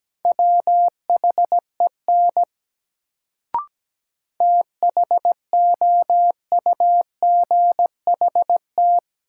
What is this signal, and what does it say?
Morse code, heard as a keyed tone, about 17 wpm, 700 Hz. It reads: WHEN THOUGHT
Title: Sets of 2 Words - New Word Drill - 17wpm